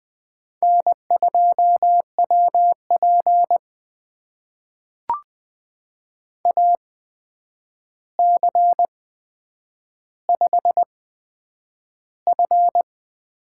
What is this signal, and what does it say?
Morse code, heard as a keyed tone, about 20 wpm, spaced as for 9 wpm, 700 Hz. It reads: N2WP AC5F
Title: USA Call Signs - 4 Characters - 8x Character Spacing - 20wpm